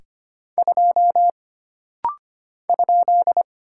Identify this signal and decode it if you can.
Morse code, heard as a keyed tone, about 25 wpm, 700 Hz. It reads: 2 ?
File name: Single Character (Letter:Number:Punctuation:Prosign) - Rapid-Fire - 25wpm